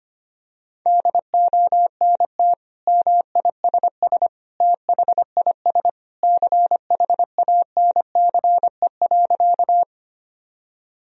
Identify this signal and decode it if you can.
Morse code, heard as a keyed tone, about 25 wpm, 700 Hz. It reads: DONT MISS THIS CHANCE.
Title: Sentences from Top 1000 Words - Review Part 3 - 25wpm